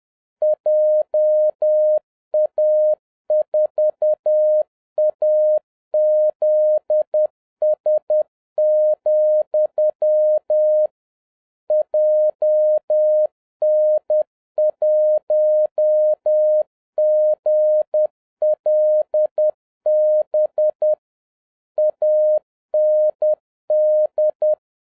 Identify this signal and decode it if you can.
Morse code, heard as a keyed tone, about 10 wpm, 610 Hz. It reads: JA4AZS, JN1GLB AND
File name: 2011-07-mp3 / n46 JOHN K3TN 10wpm